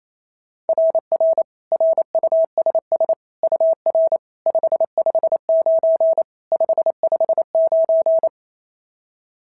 Morse code, heard as a keyed tone, about 28 wpm, 690 Hz. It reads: RR RUSS UR 559 559
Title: SOTA QSO Practice 28wpm